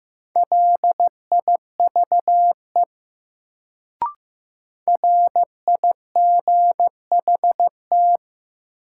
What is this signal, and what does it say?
Morse code, heard as a keyed tone, about 15 wpm, 700 Hz. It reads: LIVE RIGHT
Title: Sentences from Top 200 Words - New Word Drill - 15wpm